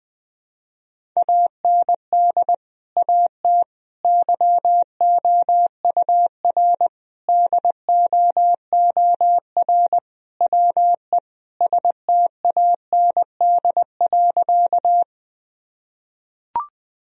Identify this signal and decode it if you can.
Morse code, heard as a keyed tone, about 20 wpm, 700 Hz. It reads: AND AT YOUR DOOR WE STAND.
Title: Sentences from Top 1000 Words - Review Part 1 - 20wpm